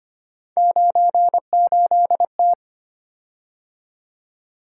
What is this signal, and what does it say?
Morse code, heard as a keyed tone, about 25 wpm, 700 Hz. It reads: 98T
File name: Three Letter-Number Combinations - 25wpm